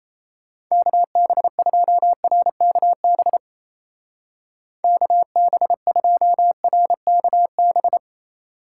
Morse code, heard as a keyed tone, about 34 wpm, 700 Hz. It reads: KB2RKB KB2RKB
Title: Speed Racing - USA 2x3 Call Signs 28wpm